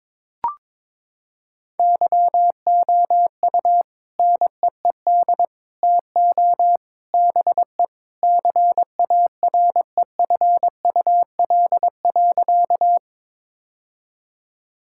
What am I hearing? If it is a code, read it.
Morse code, heard as a keyed tone, about 22 wpm, 700 Hz. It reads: YOU NEED TO BE CAREFUL.
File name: practice / 22wpm / Sentences with 3-Word Phrases - 22wpm